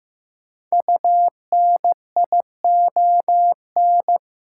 Morse code, heard as a keyed tone, about 15 wpm, 700 Hz. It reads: UNION
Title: Sentences from Top 1400 Words - New Word Drill - 15wpm